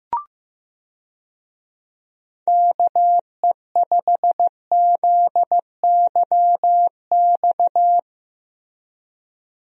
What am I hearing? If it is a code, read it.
Morse code, heard as a keyed tone, about 16 wpm, 700 Hz. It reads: KE5ZYX
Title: USA Call Signs - 6 Characters - Repeated 2x - 15wpm